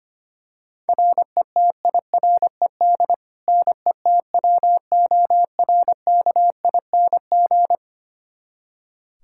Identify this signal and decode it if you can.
Morse code, heard as a keyed tone, about 25 wpm, 700 Hz. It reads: RETIRED NETWORKING
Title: K1USN SST QSO Practice 25wpm